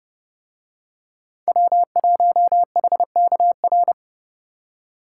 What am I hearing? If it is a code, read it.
Morse code, heard as a keyed tone, about 29 wpm, 700 Hz. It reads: W1HKR